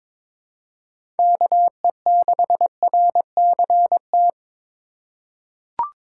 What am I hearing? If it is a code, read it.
Morse code, heard as a keyed tone, about 22 wpm, 700 Hz. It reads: KE6RCT
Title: USA Call Signs - 6 Characters - 3x Character Spacing - 22wpm